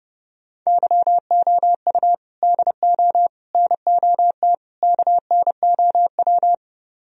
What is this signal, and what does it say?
Morse code, heard as a keyed tone, about 38 wpm, 700 Hz. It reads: YOU DO NOT KNOW